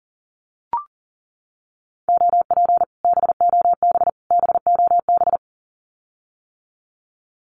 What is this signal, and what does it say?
Morse code, heard as a keyed tone, about 40 wpm, 700 Hz. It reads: OP BOB BOB
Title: Common QSO Phrases Encore 40wpm